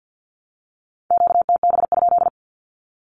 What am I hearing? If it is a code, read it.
Morse code, heard as a keyed tone, about 50 wpm, 700 Hz. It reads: QTB?
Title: Q-Codes Repeated 3x 50wpm